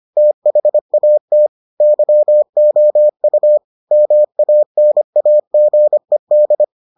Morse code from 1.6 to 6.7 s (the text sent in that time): YOU MANAGED